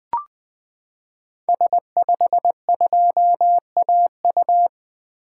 Morse code, heard as a keyed tone, about 21 wpm, 700 Hz. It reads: S52AU
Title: 2023 SOTA Activator Call Signs - 20wpm